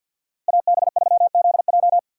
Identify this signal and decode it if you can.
Morse code, heard as a keyed tone, about 51 wpm, 700 Hz. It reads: AB3ZJ
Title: USA Call Signs - 5 Characters - 2x Character Spacing - 50wpm